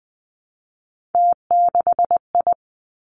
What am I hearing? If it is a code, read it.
Morse code, heard as a keyed tone, about 20 wpm, 700 Hz. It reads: T6I
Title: Three Challenging Letter-Number Combinations 20wpm